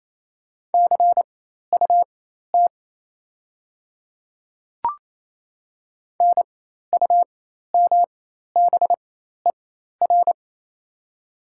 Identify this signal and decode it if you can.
Morse code, heard as a keyed tone, about 28 wpm, spaced as for 13 wpm, 700 Hz. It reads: CUT NUMBER
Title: 4x Character Spacing - Top 300 Words - 28wpm